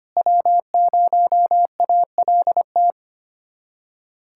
Morse code, heard as a keyed tone, about 25 wpm, 700 Hz. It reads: W0ALT
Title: USA Callsign DE Callsign - 5 Characters - 25wpm